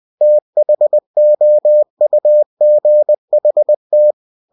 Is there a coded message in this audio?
Yes: Morse code from 0.2 to 4.1 s, heard as a keyed tone, about 20 wpm, 600 Hz. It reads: THOUGHT